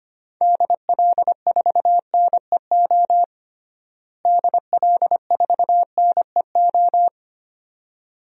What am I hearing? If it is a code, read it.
Morse code, heard as a keyed tone, about 25 wpm, 700 Hz. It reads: DL4NEO DL4NEO
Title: DXCC-100 Call Signs - 6 Characters - Repeated 2x - 25wpm